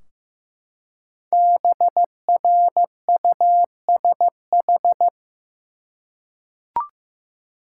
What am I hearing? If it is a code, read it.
Morse code, heard as a keyed tone, about 15 wpm, 700 Hz. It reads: BRUSH
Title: Five Letter Words - 15wpm